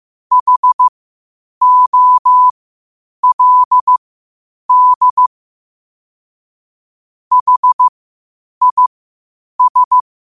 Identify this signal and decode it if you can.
Morse code, heard as a keyed tone, about 15 wpm, spaced as for 8 wpm, 1000 Hz. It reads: HOLD HIS
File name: qotdfiles / 8wpm